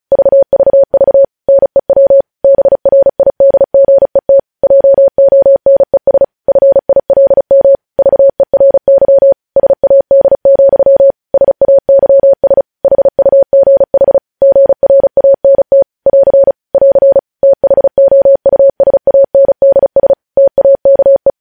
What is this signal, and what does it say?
Morse code, heard as a keyed tone, about 36 wpm, 550 Hz. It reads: VVV NEW BRIDGET JONES FILM VERY SAD, SAYS HUGH GRANT <AR> <AR> THOUSANDS TAKE